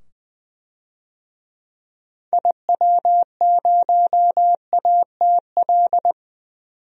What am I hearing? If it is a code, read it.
Morse code, heard as a keyed tone, about 20 wpm, 700 Hz. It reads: IW0ATL
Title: DXCC-100 Call Signs - 6 Characters - 20wpm